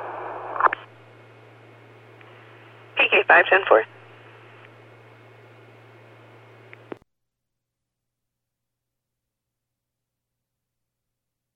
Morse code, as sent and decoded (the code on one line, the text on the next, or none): none